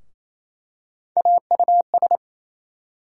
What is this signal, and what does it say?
Morse code, heard as a keyed tone, about 27 wpm, 700 Hz. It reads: AUS
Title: USA Call Sign Postfixes - 3 Letters 28wpm